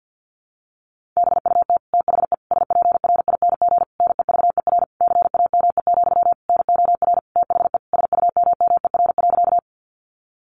Morse code, heard as a keyed tone, about 50 wpm, 700 Hz. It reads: BUT THE SPRING NEVER CAME, NOR THE SUMMER.
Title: Sentences from Top 1500 Words 50wpm